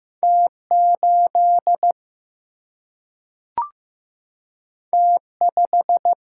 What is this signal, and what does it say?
Morse code, heard as a keyed tone, about 15 wpm, 700 Hz. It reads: T 8 T 5
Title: Single-Letter Single-Number 15wpm